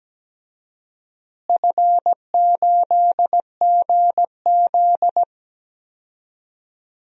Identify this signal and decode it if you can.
Morse code, heard as a keyed tone, about 17 wpm, 700 Hz. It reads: F8GZ
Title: DXCC-100 Call Signs - 4 Characters - Repeated 2x - 17wpm